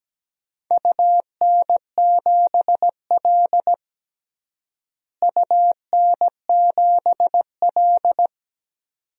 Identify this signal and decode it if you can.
Morse code, heard as a keyed tone, about 17 wpm, 700 Hz. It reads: UN7L UN7L
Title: DXCC-100 Call Signs - 4 Characters - Repeated 2x - 17wpm